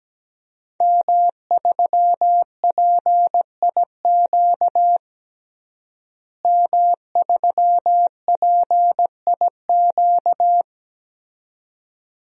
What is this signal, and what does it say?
Morse code, heard as a keyed tone, about 17 wpm, 700 Hz. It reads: M3PIQ M3PIQ
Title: DXCC-100 Call Signs - 5 Characters - Repeated 2x - 17wpm